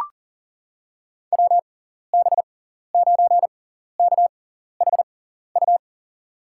Morse code, heard as a keyed tone, about 38 wpm, spaced as for 14 wpm, 700 Hz. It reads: WB9KHU
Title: USA Call Signs - 6 Characters - 6x Character Spacing - 40wpm